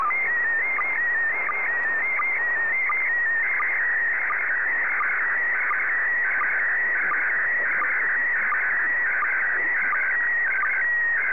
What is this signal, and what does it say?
SSTV (slow-scan television), PD 90.